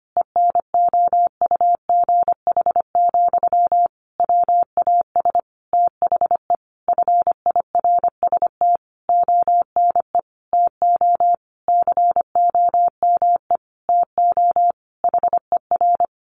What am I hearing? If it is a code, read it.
Morse code, heard as a keyed tone, about 25 wpm, 700 Hz. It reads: ENOUGH, WAS THE FIRST ONE TO COME TO HER
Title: Sentences from Top 1300 Words 25wpm